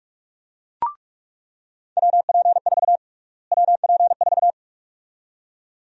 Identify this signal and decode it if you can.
Morse code, heard as a keyed tone, about 46 wpm, 700 Hz. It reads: WP4 WP4
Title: USA Call Sign Prefixes - Repeated 2x - 45wpm